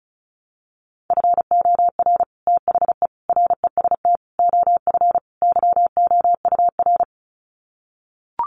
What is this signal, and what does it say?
Morse code, heard as a keyed tone, about 34 wpm, 700 Hz. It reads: FOR THE REST OF YOUR
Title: Sets of 5 Words Encore 35wpm